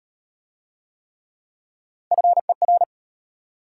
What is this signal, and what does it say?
Morse code, heard as a keyed tone, about 32 wpm, 690 Hz. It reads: FER